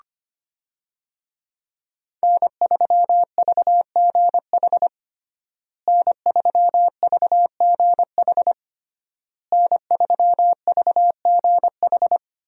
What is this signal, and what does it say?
Morse code, heard as a keyed tone, about 25 wpm, 700 Hz. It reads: N3VGH N3VGH N3VGH
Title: USA Call Signs - 5 Characters - Repeated 3x - 25wpm